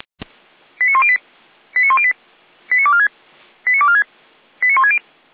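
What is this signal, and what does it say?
Five-tone selective call (ZVEI), heard as quick runs of five tones, tones of about 80 ms. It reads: 89108 89108 89246 89246 89160
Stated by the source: ZVEI five-tone selective call received by SDR.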